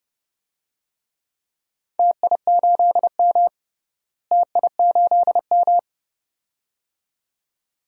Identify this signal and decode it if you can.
Morse code, heard as a keyed tone, about 31 wpm, 700 Hz. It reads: TI8M TI8M
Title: DXCC-100 Call Signs - 4 Characters - Repeated 2x - 30wpm